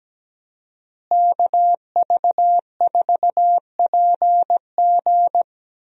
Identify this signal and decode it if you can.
Morse code, heard as a keyed tone, about 17 wpm, 700 Hz. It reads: KV4PG
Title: USA Call Signs - 5 Characters - Repeated 3x - 17wpm